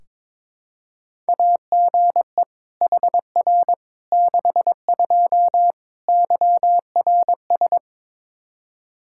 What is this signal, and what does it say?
Morse code, heard as a keyed tone, about 22 wpm, 700 Hz. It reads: AGE HR 62 YRS
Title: Practice QSO - Exchange 3 - More Info - 22wpm